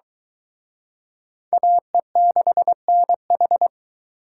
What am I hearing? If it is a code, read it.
Morse code, heard as a keyed tone, about 22 wpm, 700 Hz. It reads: AE6NH